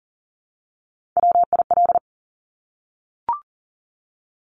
Morse code, heard as a keyed tone, about 38 wpm, 700 Hz. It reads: WIL
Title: Top Three Letter Combinations in Words 40wpm